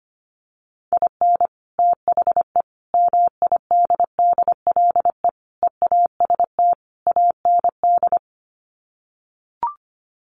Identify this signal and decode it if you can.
Morse code, heard as a keyed tone, about 25 wpm, 700 Hz. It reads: IN THE MIDDLE EAST AND
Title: Sets of 5 Words Encore 25wpm